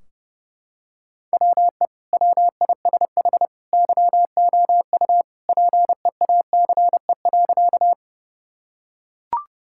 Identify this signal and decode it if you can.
Morse code, heard as a keyed tone, about 30 wpm, 700 Hz. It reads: WE WISH YOU PEACE.
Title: Sentences from Top 1600 Words 30wpm